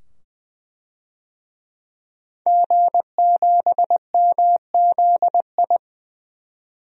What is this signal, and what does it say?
Morse code, heard as a keyed tone, about 20 wpm, 700 Hz. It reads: G7MZI
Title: DXCC-100 Call Signs - 5 Characters - 20wpm